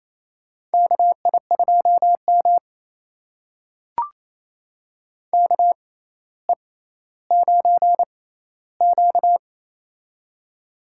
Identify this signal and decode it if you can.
Morse code, heard as a keyed tone, about 28 wpm, spaced as for 14 wpm, 700 Hz. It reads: KI2M KE9Q
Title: USA Call Signs - 4 Characters - 6x Character Spacing - 28wpm